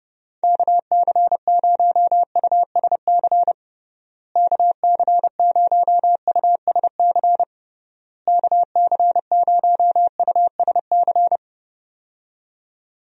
Morse code, heard as a keyed tone, about 29 wpm, 700 Hz. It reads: KC0USC KC0USC KC0USC
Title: USA Call Signs - 6 Characters - Repeated 3x - 30wpm